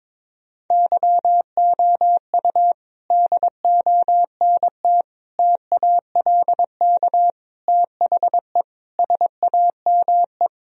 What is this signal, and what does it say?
Morse code, heard as a keyed tone, about 22 wpm, 700 Hz. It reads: YOU DONT TALK THE SAME
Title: Sentences from Top 1000 Words - Review Part 3 - 22wpm